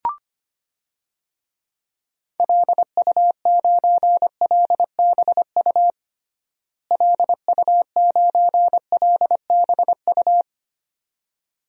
Morse code, heard as a keyed tone, about 25 wpm, 700 Hz. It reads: LU9LBU LU9LBU
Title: DXCC-100 Call Signs - 6 Characters - Repeated 2x - 25wpm